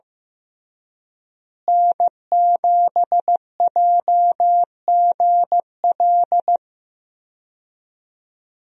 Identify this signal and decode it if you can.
Morse code, heard as a keyed tone, about 15 wpm, 700 Hz. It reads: N7JGL
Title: USA Call Signs - 5 Characters - Repeated 3x - 15wpm